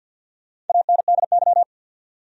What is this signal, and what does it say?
Morse code, heard as a keyed tone, about 46 wpm, 700 Hz. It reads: ANDY
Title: USA Common Names - Max 5 Letters - 50wpm